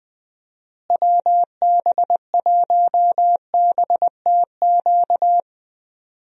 Morse code, heard as a keyed tone, about 20 wpm, 700 Hz. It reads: WB1BTQ
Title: USA Call Signs - 6 Characters - Repeated 3x - 20wpm